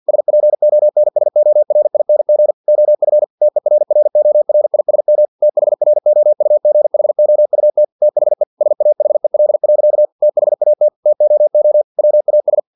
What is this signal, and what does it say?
Morse code, heard as a keyed tone, about 50 wpm, 600 Hz. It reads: SPONSORING OF TERRORISM THROUGHOUT THE SAHEL. THAT TOO WAS
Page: Never 49wpm